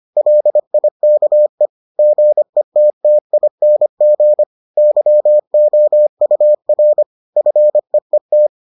Morse code, heard as a keyed tone, about 25 wpm, 600 Hz. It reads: LIKE GETTING YOUR FEET